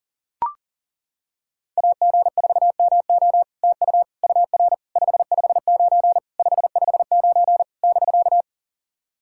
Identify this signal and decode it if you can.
Morse code, heard as a keyed tone, about 40 wpm, 700 Hz. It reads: AG4MO TU UR 559 559 <BK>
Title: SOTA QSO Practice 40wpm